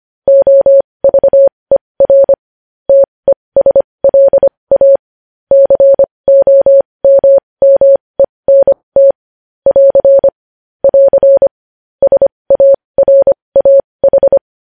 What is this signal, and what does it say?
Morse code, heard as a keyed tone, about 25 wpm, spaced as for 20 wpm, 550 Hz. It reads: OVER TESLA COMMENT <AR> <AR> SARAH
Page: news headlines in morse code at 20 wpm